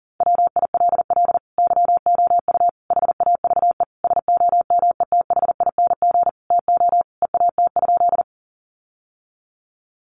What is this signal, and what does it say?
Morse code, heard as a keyed tone, about 40 wpm, 700 Hz. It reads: WILL YOU HAVE SOMETHING TO EAT?
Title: Sentences from Top 600 Words 40wpm